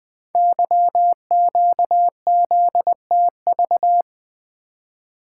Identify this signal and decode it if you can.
Morse code, heard as a keyed tone, about 20 wpm, 700 Hz. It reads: YQZTV
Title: Code Groups - 5 Letter:Numbers - 20wpm